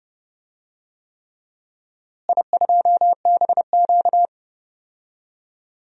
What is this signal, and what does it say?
Morse code, heard as a keyed tone, about 31 wpm, 700 Hz. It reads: I2BQ
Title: DXCC-100 Call Signs - 4 Characters - Repeated 3x - 30wpm